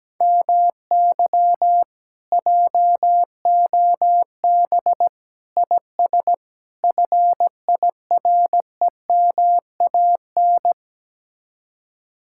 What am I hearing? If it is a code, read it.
Morse code, heard as a keyed tone, about 17 wpm, 700 Hz. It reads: MY JOB IS FIREMAN
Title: Speed-Racing Common QSO Phrases - 17wpm